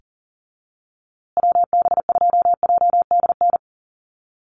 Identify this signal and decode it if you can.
Morse code, heard as a keyed tone, about 40 wpm, 700 Hz. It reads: WB2JDN
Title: USA Call Signs Repeated 2x 40wpm